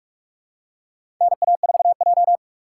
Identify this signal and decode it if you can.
Morse code, heard as a keyed tone, about 46 wpm, 690 Hz. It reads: NA4J
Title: USA Call Signs - 4 Characters - 3x Character Spacing - 45wpm